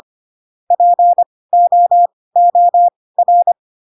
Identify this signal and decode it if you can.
Morse code, heard as a keyed tone, about 25 wpm, spaced as for 18 wpm, 700 Hz. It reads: POOR
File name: Pinocchio / Pinocchio ch5 12 25 ik1hge